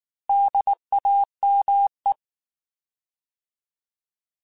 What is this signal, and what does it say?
Morse code, heard as a keyed tone, about 19 wpm, 800 Hz. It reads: DAME
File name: Alphabetical / Alphabetical mp3s / Ds @18